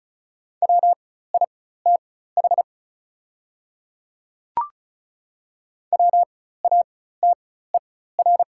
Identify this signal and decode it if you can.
Morse code, heard as a keyed tone, about 36 wpm, spaced as for 16 wpm, 690 Hz. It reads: WITH WATER
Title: 4x Character Spacing - Top 200 Words - 35wpm